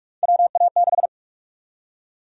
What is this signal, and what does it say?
Morse code, heard as a keyed tone, about 46 wpm, 700 Hz. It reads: WA6
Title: USA Call Sign Prefixes - Repeated 2x - 45wpm